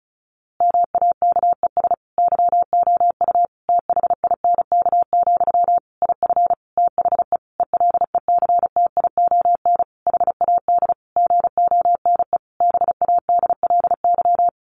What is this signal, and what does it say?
Morse code, heard as a keyed tone, about 34 wpm, 700 Hz. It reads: MAKES YOU THINK, IF THE ELECTION HAD GONE BADLY
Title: A Sound Of Thunder 35wpm